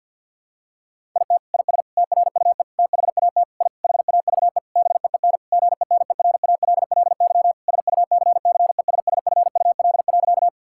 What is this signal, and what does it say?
Morse code, heard as a keyed tone, about 50 wpm, 700 Hz. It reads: IT IS TRUE THAT I HAVE BEEN GENERALLY SUCCESSFUL.
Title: Sentences from Top 1000 Words - Review Part 6 - 50wpm